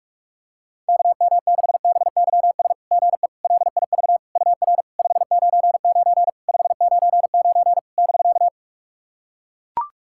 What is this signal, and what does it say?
Morse code, heard as a keyed tone, about 44 wpm, 690 Hz. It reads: KM6BYS GE LIV UR 599 599 <BK>
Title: SOTA QSO Practice 45wpm